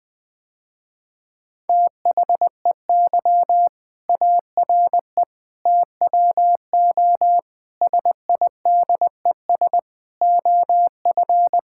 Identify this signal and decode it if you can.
Morse code, heard as a keyed tone, about 20 wpm, 700 Hz. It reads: THEY ARE TWO SIDES OF
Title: Sentences with 7-Word Phrases - Spoken Beginning - 20wpm